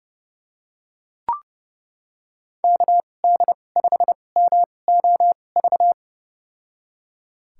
Morse code, heard as a keyed tone, about 31 wpm, spaced as for 22 wpm, 700 Hz. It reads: KD5MOV
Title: USA Call Signs - 6 Characters - 2x Character Spacing - 30wpm